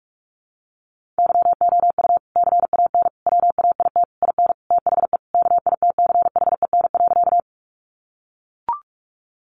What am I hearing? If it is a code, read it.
Morse code, heard as a keyed tone, about 44 wpm, 700 Hz. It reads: YOU CAN WAIT IN THE KITCHEN.
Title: Sentences from Top 1400 Words 45wpm